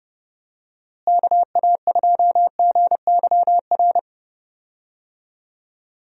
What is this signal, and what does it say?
Morse code, heard as a keyed tone, about 29 wpm, 700 Hz. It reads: KA2GYR
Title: USA Call Signs - 6 Characters - Repeated 3x - 30wpm